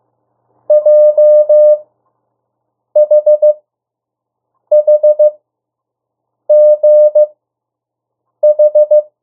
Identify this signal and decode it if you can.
Morse code, heard as a keyed tone, about 15 wpm, spaced as for 6 wpm, 600 Hz. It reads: JHHGH
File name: urok 1